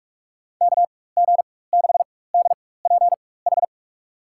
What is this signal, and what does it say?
Morse code, heard as a keyed tone, about 42 wpm, spaced as for 21 wpm, 700 Hz. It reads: KG6DPH
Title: USA Call Signs - 6 Characters - 4x Character Spacing - 45wpm